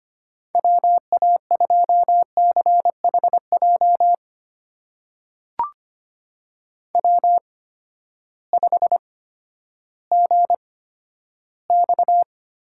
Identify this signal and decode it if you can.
Morse code, heard as a keyed tone, about 25 wpm, spaced as for 12 wpm, 700 Hz. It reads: WA2CHJ W5GX
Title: USA Call Signs - 8x Character Spacing - 25wpm